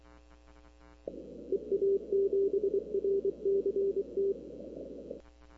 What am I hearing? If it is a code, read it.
Morse code, heard as a keyed tone, about 24 wpm, 390 Hz. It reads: EA7RCT
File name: k5tr arrl cw dx